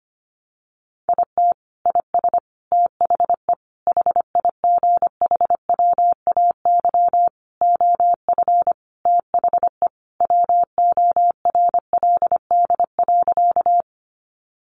Morse code, heard as a keyed tone, about 25 wpm, 700 Hz. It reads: IT IS THE HIGHWAY OF THE WORLD.
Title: Sentences from Top 1400 Words 25wpm